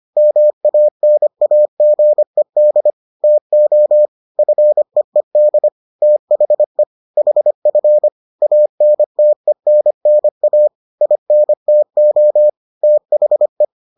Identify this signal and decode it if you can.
Morse code, heard as a keyed tone, about 25 wpm, 600 Hz. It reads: MANAGED TO FEED THE HF ANTENNA INTO THE